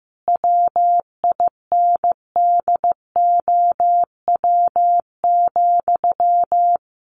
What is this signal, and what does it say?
Morse code, heard as a keyed tone, about 15 wpm, 700 Hz. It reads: WINDOW,
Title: A Sound Of Thunder 15wpm